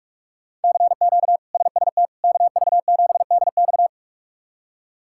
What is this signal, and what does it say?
Morse code, heard as a keyed tone, about 47 wpm, 700 Hz. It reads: CQ SST KV7DX
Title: K1USN SST QSO Practice - 45wpm